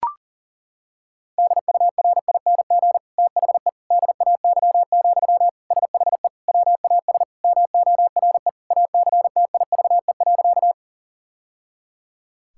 Morse code, heard as a keyed tone, about 40 wpm, 700 Hz. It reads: DURING THE DAY, SHE WAS MORE ACTIVE.
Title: Sentences from Top 1000 Words - Review Part 6 - 40wpm